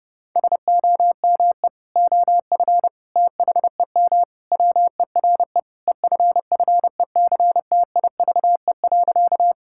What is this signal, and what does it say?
Morse code, heard as a keyed tone, about 30 wpm, 700 Hz. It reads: SOME OF THEM WERE EFFECTIVE.
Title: Sentences from Top 1000 Words - Review Part 6 - 30wpm